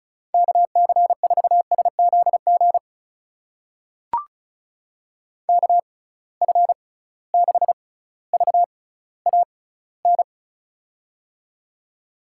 Morse code, heard as a keyed tone, about 36 wpm, spaced as for 18 wpm, 700 Hz. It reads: KC4SZG KF6VAN
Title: USA Call Signs - 6 Characters - 6x Character Spacing - 35wpm